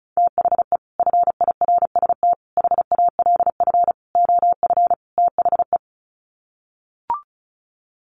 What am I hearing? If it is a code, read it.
Morse code, heard as a keyed tone, about 34 wpm, 700 Hz. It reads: THE FIRST HALF OF THE
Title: Sets of 5 Words 35wpm